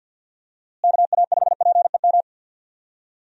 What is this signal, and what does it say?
Morse code, heard as a keyed tone, about 52 wpm, 700 Hz. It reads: KA5PEM
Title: USA Call Signs - 6 Characters - Repeated 2x - 50wpm